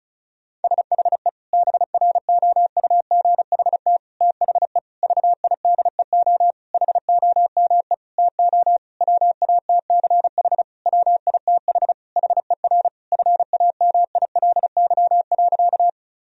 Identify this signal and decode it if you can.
Morse code, heard as a keyed tone, about 34 wpm, 700 Hz. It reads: SHE BROUGHT THE VIDEO HOME TO WATCH WITH HER FAMILY.
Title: Sentences from Top 1000 Words - Review Part 4 - 35wpm